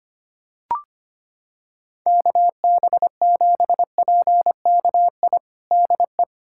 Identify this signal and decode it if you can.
Morse code, heard as a keyed tone, about 25 wpm, 700 Hz. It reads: KB7PKI DE
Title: USA Callsign DE Callsign - 6 Characters - 25wpm